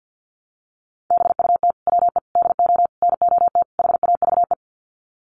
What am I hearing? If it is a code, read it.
Morse code, heard as a keyed tone, about 50 wpm, 700 Hz. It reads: BUT WE DO NOT HAVE
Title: Sets of 5 Words Encore2 50wpm